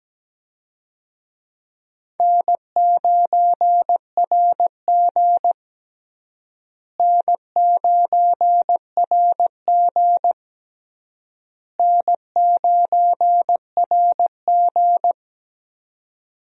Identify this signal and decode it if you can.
Morse code, heard as a keyed tone, about 17 wpm, 700 Hz. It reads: N9RG N9RG N9RG
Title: USA Call Signs - 4 Characters - Repeated 3x - 17wpm